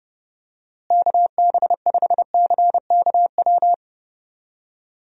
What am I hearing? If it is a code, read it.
Morse code, heard as a keyed tone, about 32 wpm, 700 Hz. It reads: KB5CKW
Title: USA Call Signs - 6 Characters - Repeated 2x - 30wpm